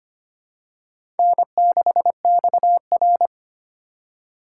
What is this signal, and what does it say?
Morse code, heard as a keyed tone, about 25 wpm, 700 Hz. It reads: N6XR
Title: USA Call Signs - 4 Characters - 8x Character Spacing - 25wpm